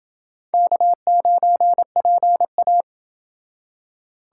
Morse code, heard as a keyed tone, about 27 wpm, 700 Hz. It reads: K9PA